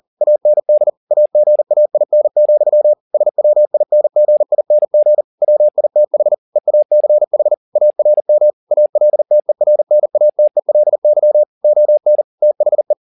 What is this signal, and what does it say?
Morse code, heard as a keyed tone, about 40 wpm, 600 Hz. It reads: AND AGAIN, SWINGING WITH EACH ARM ALTERNATELY ON THE